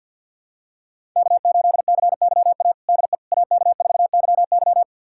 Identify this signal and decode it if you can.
Morse code, heard as a keyed tone, about 49 wpm, 700 Hz. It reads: K8CYA DE AK4YY
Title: USA Callsign DE Callsign - 5 Characters - 50wpm